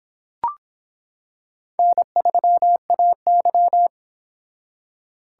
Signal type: Morse code

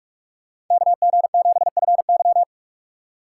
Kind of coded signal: Morse code

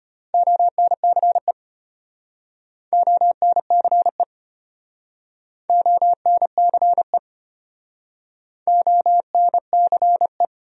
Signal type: Morse code

